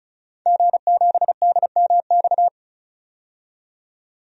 Morse code, heard as a keyed tone, about 34 wpm, 700 Hz. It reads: G7DMX
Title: DXCC-100 Call Signs - 5 Characters - 35wpm